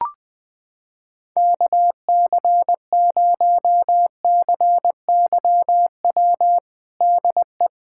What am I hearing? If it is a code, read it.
Morse code, heard as a keyed tone, about 21 wpm, 700 Hz. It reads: KC0CYW DE
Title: USA Callsign DE Callsign - 6 Characters - 20wpm